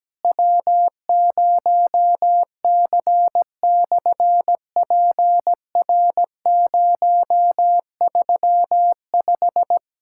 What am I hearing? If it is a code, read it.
Morse code, heard as a keyed tone, about 17 wpm, 700 Hz. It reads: W0C/PR035
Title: SOTA QSO Practice 17wpm